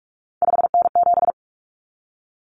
Morse code, heard as a keyed tone, about 44 wpm, 700 Hz. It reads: 5N7
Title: Common Cut Numbers - 3-Digit 45wpm